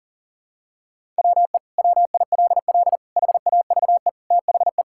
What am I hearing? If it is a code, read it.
Morse code, heard as a keyed tone, about 40 wpm, 700 Hz. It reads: WE WILL HAVE THE
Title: Sentences from Top 1000 Words - Review Part 6 - 40wpm